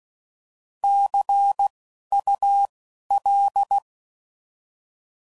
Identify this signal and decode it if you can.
Morse code, heard as a keyed tone, about 16 wpm, 800 Hz. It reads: C U L